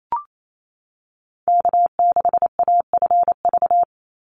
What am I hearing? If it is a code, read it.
Morse code, heard as a keyed tone, about 28 wpm, 700 Hz. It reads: K6AFV
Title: USA Call Signs 28wpm